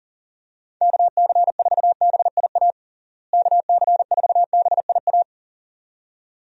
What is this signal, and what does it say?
Morse code, heard as a keyed tone, about 37 wpm, 700 Hz. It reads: KC4BIA KC4BIA
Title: USA Call Signs - 6 Characters - Repeated 2x - 40wpm